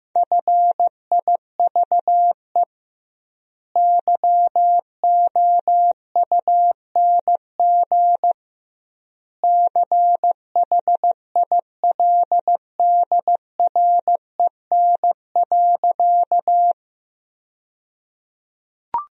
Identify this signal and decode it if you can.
Morse code, heard as a keyed tone, about 15 wpm, 700 Hz. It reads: FIVE YOUNG CHILDREN.
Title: 2x Word-Spacing - Sentences from Top 400 Words - 15wpm